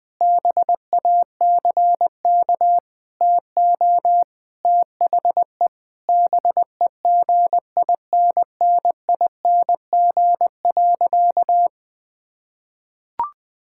Morse code, heard as a keyed tone, about 20 wpm, 700 Hz. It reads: BACK TO THE BEGINNING.
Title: Sentences with 3-Word Phrases - 20wpm